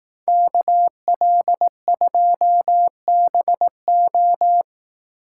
Morse code, heard as a keyed tone, about 17 wpm, 700 Hz. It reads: KL2BO
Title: Speed Racing - USA 2x2 Call Signs 15wpm